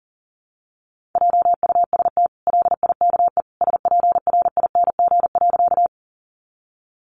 Morse code, heard as a keyed tone, about 40 wpm, 700 Hz. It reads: JUST LIKE SPRING.
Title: Sentences from Top 1200 Words 40wpm